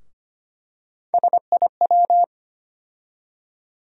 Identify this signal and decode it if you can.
Morse code, heard as a keyed tone, about 25 wpm, 700 Hz. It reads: SIW
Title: USA Call Sign Postfixes - 3 Letters 25wpm